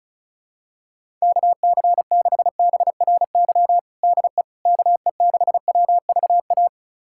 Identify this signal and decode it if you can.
Morse code, heard as a keyed tone, about 36 wpm, 700 Hz. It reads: KC6BRY DE KE6WVA
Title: USA Callsign DE Callsign - 6 Characters - 35wpm